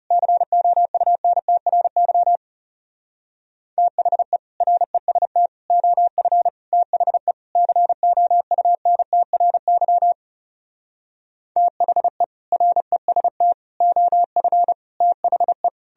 Morse code, heard as a keyed tone, about 40 wpm, 700 Hz. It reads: COUNTRY THE REST OF THE COUNTRY THE REST OF THE